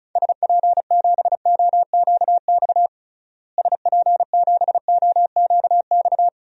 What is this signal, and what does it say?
Morse code, heard as a keyed tone, about 36 wpm, 700 Hz. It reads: SP7OQX SP7OQX
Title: DXCC-100 Call Signs - 6 Characters - Repeated 2x - 35wpm